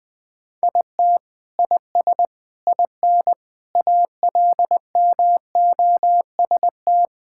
Morse code, heard as a keyed tone, about 20 wpm, 700 Hz. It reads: IT IS IN ALMOST
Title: Sentences from Top 1000 Words - Review Part 5 - 20wpm